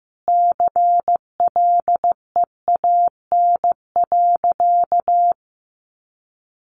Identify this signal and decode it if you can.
Morse code, heard as a keyed tone, about 15 wpm, 700 Hz. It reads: CLEAN.
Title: Sentences from Top 1100 Words 15wpm